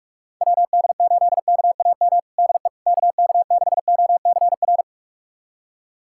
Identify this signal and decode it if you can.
Morse code, heard as a keyed tone, about 46 wpm, 700 Hz. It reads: WD8KAM DE KK6OCR
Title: USA Callsign DE Callsign 45wpm